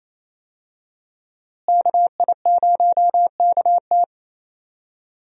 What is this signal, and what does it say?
Morse code, heard as a keyed tone, about 28 wpm, 700 Hz. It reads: KI0KT
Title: USA Call Signs - 5 Characters - Repeated 3x - 28wpm